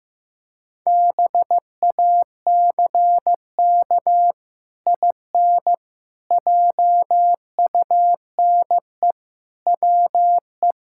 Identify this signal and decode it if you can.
Morse code, heard as a keyed tone, about 15 wpm, 700 Hz. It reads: BACK IN JUNE WE
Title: Pangram Challenge v2 - 15wpm